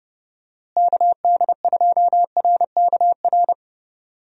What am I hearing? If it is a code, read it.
Morse code, heard as a keyed tone, about 30 wpm, 700 Hz. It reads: KD2RKR
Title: USA Call Signs - 6 Characters - Repeated 2x - 30wpm